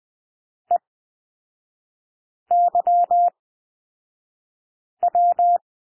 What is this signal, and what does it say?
Morse code, heard as a keyed tone, about 20 wpm, spaced as for 5 wpm, 700 Hz. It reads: EYW
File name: cwmp3 / L2 SOUND LETTERS 20 wpm